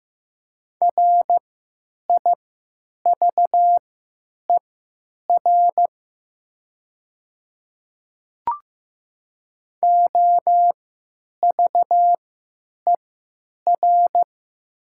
Morse code, heard as a keyed tone, about 15 wpm, spaced as for 9 wpm, 700 Hz. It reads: RIVER OVER
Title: 3x Character Spacing - Top 500 Words - 15wpm